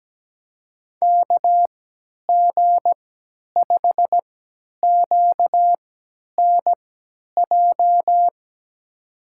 Morse code, heard as a keyed tone, about 17 wpm, spaced as for 10 wpm, 700 Hz. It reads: KG5QNJ
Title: USA Call Signs - 6 Characters - 3x Character Spacing - 17wpm